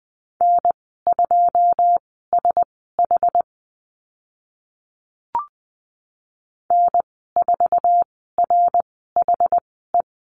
Morse code, heard as a keyed tone, about 20 wpm, spaced as for 15 wpm, 700 Hz. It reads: N2SH N4RHE
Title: 082 Lesson 14g - Number 2 - Call Signs 20wpm